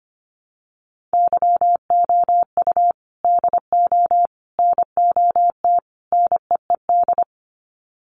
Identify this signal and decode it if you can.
Morse code, heard as a keyed tone, about 25 wpm, 700 Hz. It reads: YOU DO NOT NEED
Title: Sets of 4 Words 25wpm